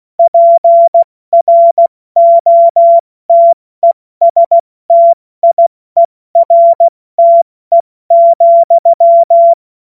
Morse code, heard as a keyed tone, about 16 wpm, spaced as for 14 wpm, 680 Hz. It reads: PROTESTIERTE,
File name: Morsekurs / Witze / Witz 25. Oktober 2022 70 F80